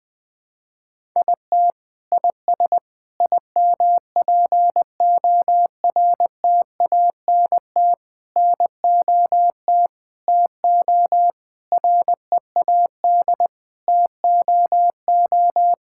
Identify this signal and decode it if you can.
Morse code, heard as a keyed tone, about 20 wpm, 700 Hz. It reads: IT IS IMPORTANT NOT TO READ TOO
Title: Sets of 9 Words - 20wpm